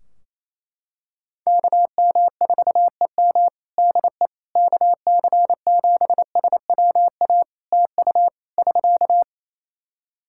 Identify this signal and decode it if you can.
Morse code, heard as a keyed tone, about 28 wpm, 700 Hz. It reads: KM4EM DE KC7SWA TU <SK>
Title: Practice QSO - Exchange 4 - Ending - 28wpm